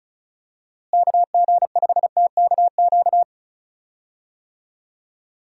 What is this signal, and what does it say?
Morse code, heard as a keyed tone, about 36 wpm, 700 Hz. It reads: KG5TKQ
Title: USA Call Signs - 6 Characters - Repeated 2x - 35wpm